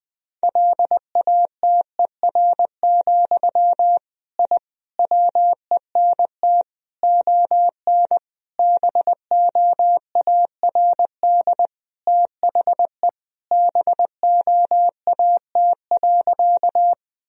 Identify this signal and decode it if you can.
Morse code, heard as a keyed tone, about 20 wpm, 700 Hz. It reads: LATER, I WENT ON BOARD THE BOAT.
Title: Sentences from Top 1000 Words - Review Part 4 - 20wpm